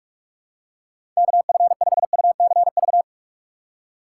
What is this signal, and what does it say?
Morse code, heard as a keyed tone, about 47 wpm, 700 Hz. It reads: KF5UCV